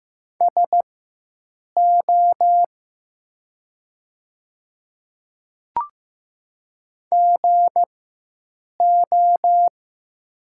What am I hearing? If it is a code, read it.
Morse code, heard as a keyed tone, about 15 wpm, spaced as for 7 wpm, 700 Hz. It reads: SO GO
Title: 4x Character Spacing - Top 100 Words - 15wpm